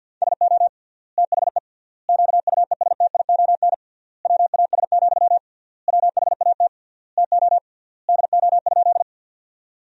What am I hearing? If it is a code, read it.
Morse code, heard as a keyed tone, about 50 wpm, 700 Hz. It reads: SO THE QUESTION WAS, WHAT TO DO?
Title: 3x Word-Spacing - Sentences from Top 400 Words - 50wpm